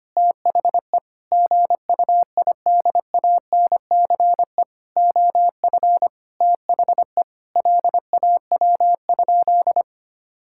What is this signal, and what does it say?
Morse code, heard as a keyed tone, about 25 wpm, 700 Hz. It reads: THE GUIDANCE OF THE LAW?
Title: Sentences from Top 1600 Words 25wpm